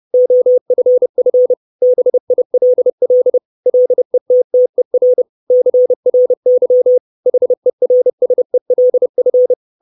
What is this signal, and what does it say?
Morse code, heard as a keyed tone, about 30 wpm, 500 Hz. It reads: OFF BILL LETTER CRY HERSELF
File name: 632 Words 30WPM